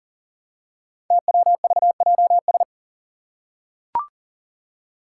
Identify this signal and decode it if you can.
Morse code, heard as a keyed tone, about 40 wpm, 700 Hz. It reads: TWVJS
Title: Five Challenging Letter-Number Combinations 40wpm